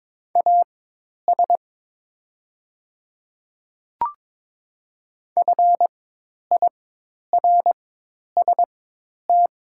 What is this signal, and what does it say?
Morse code, heard as a keyed tone, about 22 wpm, spaced as for 10 wpm, 700 Hz. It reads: AS FIRST
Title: 4x Character Spacing - Top 100 Words - 22wpm